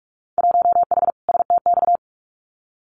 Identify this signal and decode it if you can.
Morse code, heard as a keyed tone, about 46 wpm, 700 Hz. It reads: 1H STX
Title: Winter Field Day Exchanges 45wpm